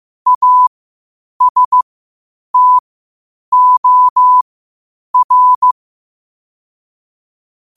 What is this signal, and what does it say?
Morse code, heard as a keyed tone, about 15 wpm, spaced as for 8 wpm, 1000 Hz. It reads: ASTOR
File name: qotdfiles / 8wpm